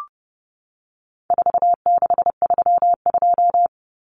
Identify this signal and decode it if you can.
Morse code, heard as a keyed tone, about 30 wpm, 700 Hz. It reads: 4632
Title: Common Cut Numbers - 4-Digit 30wpm